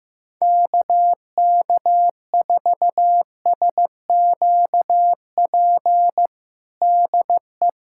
Morse code, heard as a keyed tone, about 15 wpm, 700 Hz. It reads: KK4SQP DE
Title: USA Callsign DE Callsign 15wpm